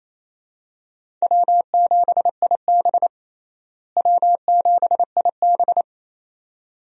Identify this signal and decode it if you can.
Morse code, heard as a keyed tone, about 28 wpm, 700 Hz. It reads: W7IB W7IB
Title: USA Call Signs - 4 Characters - Repeated 2x - 28wpm